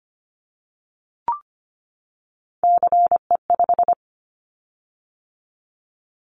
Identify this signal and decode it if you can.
Morse code, heard as a keyed tone, about 25 wpm, 700 Hz. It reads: CE5
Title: Callsign Prefixes for Top 100 DXCC - 25wpm